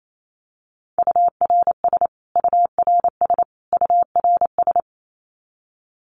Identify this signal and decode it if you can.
Morse code, heard as a keyed tone, about 28 wpm, 700 Hz. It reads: URS URS URS
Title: Abbreviations Repeated 3x - 28wpm